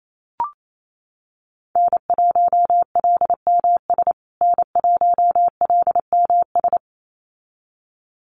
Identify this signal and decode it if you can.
Morse code, heard as a keyed tone, about 28 wpm, 700 Hz. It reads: N1LMS N1LMS
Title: USA Call Signs Repeated 2x 28wpm